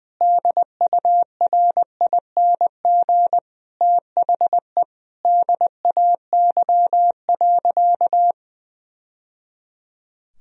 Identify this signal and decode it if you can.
Morse code, heard as a keyed tone, about 20 wpm, 700 Hz. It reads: DURING THE DAY.
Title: Sentences from Top 1000 Words - Review Part 3 - 20wpm